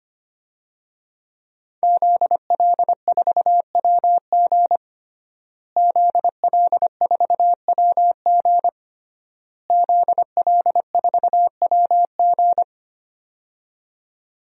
Morse code, heard as a keyed tone, about 25 wpm, 700 Hz. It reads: ZL4WG ZL4WG ZL4WG
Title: DXCC-100 Call Signs - 5 Characters - Repeated 3x - 25wpm